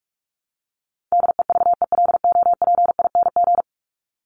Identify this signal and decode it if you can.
Morse code, heard as a keyed tone, about 44 wpm, 700 Hz. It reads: DEVELOPING
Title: ING Suffix - Top 100 Words with Base Words 45wpm